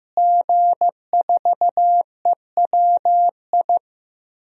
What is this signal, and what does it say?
Morse code, heard as a keyed tone, about 15 wpm, 700 Hz. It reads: G4EWI
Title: DXCC-100 Call Signs - 5 Characters - Repeated 2x - 15wpm